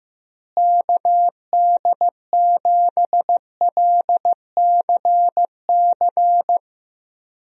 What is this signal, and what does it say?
Morse code, heard as a keyed tone, about 15 wpm, 700 Hz. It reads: KD7LCC
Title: USA Call Signs - 6 Characters - Repeated 2x - 15wpm